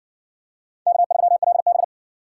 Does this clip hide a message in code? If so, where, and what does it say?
Yes: Morse code from 0.9 to 1.8 s, heard as a keyed tone, about 54 wpm, 700 Hz. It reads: K3LB